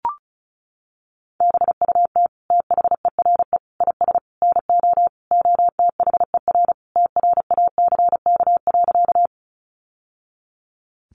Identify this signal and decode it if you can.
Morse code, heard as a keyed tone, about 34 wpm, 700 Hz. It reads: BUT THERE IS NO OTHER TRACK.
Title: Sentences from Top 1100 Words 35wpm